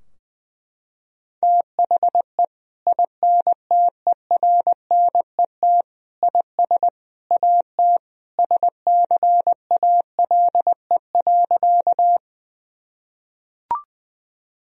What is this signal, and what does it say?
Morse code, heard as a keyed tone, about 21 wpm, 700 Hz. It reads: THE INTERNET IS AT SCALE.
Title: Sentences from Top 1000 Words - Review Part 4 - 20wpm